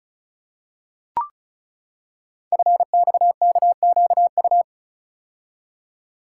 Morse code, heard as a keyed tone, about 36 wpm, 700 Hz. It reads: FXKQU
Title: Code Groups - 5 Letters - 35wpm